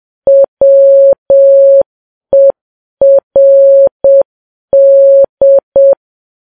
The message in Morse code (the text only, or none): WERD